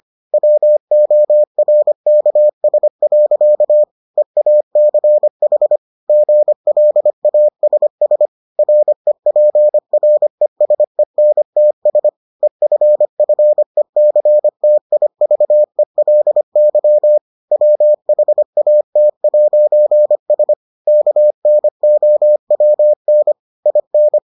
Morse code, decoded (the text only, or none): WORKS. EACH GLASS REPRESENTS EFFECTIVELY WHAT'S KNOWN IN